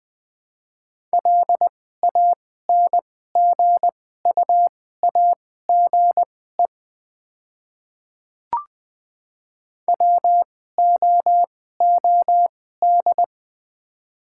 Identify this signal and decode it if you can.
Morse code, heard as a keyed tone, about 21 wpm, spaced as for 15 wpm, 700 Hz. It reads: LANGUAGE WOOD
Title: 2x Character Spacing - Top 500 Words - 20wpm